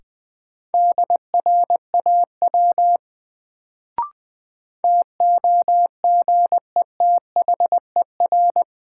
Morse code, heard as a keyed tone, about 20 wpm, 700 Hz. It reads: DRAW TOGETHER
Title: Sentences from Top 300 Words - New Word Drill - 20wpm